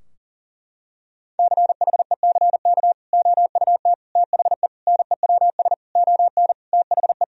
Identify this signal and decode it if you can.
Morse code, heard as a keyed tone, about 40 wpm, 700 Hz. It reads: CHECK OUT THE NEWS ON THE
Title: Sentences from Top 1000 Words - Review Part 4 - 40wpm